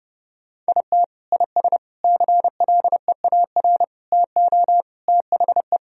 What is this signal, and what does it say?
Morse code, heard as a keyed tone, about 30 wpm, 700 Hz. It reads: IT IS CLEAR TO THE
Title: Sentences from Top 1000 Words - Review Part 2 - 30wpm